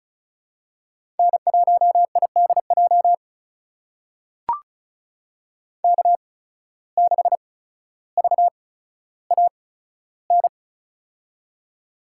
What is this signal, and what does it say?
Morse code, heard as a keyed tone, about 34 wpm, spaced as for 15 wpm, 700 Hz. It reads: N1IDJ K6VAN
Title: USA Call Signs - 5 Characters - 8x Character Spacing - 35wpm